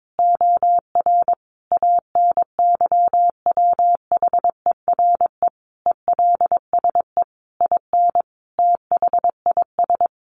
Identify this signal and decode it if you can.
Morse code, heard as a keyed tone, about 22 wpm, 700 Hz. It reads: OR ANYWHERE ELSE IN THIS
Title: Sentences from Top 1400 Words 22wpm